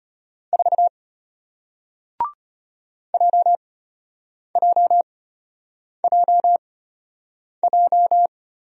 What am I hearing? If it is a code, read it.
Morse code, heard as a keyed tone, about 34 wpm, spaced as for 8 wpm, 700 Hz. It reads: 4 JJJJ